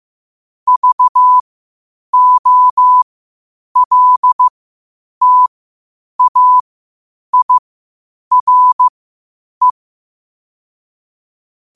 Morse code, heard as a keyed tone, about 15 wpm, spaced as for 8 wpm, 1000 Hz. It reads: VOLTAIRE